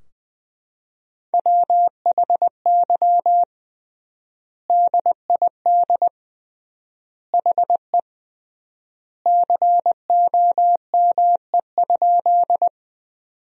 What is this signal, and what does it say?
Morse code, heard as a keyed tone, about 21 wpm, 700 Hz. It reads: WHY DID HE COME?
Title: 3x Word-Spacing - Sentences from Top 200 Words - 20wpm